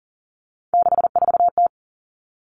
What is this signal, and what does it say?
Morse code, heard as a keyed tone, about 40 wpm, 700 Hz. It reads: 64T
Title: Common Cut Numbers - 3-Digit 40wpm